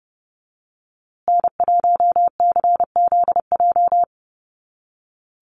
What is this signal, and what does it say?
Morse code, heard as a keyed tone, about 28 wpm, 700 Hz. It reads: N1CZJ
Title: USA Call Signs Repeated 3x 30wpm